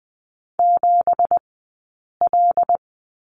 Morse code, heard as a keyed tone, about 20 wpm, 700 Hz. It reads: 7 L
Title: 190 Lesson 31d - Letter v - Two Characters 20wpm